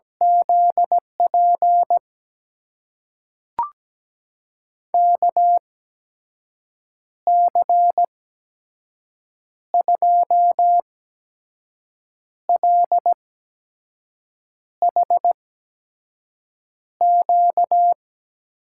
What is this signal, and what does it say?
Morse code, heard as a keyed tone, about 17 wpm, spaced as for 5 wpm, 700 Hz. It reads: ZP KC2LHQ
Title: USA Call Signs - 6 Characters - 8x Character Spacing - 17wpm